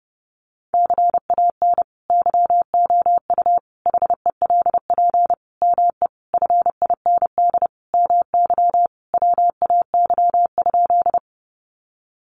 Morse code, heard as a keyed tone, about 30 wpm, 700 Hz. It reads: CAN YOU HELP ME FIND MY WAY?
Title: Sentences from Top 200 Words 30wpm